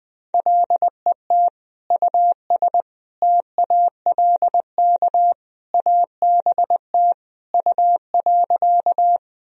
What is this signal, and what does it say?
Morse code, heard as a keyed tone, about 20 wpm, 700 Hz. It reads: LET US TALK ABT U.
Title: Sentences from Top 400 Words and Abbreviations 20wpm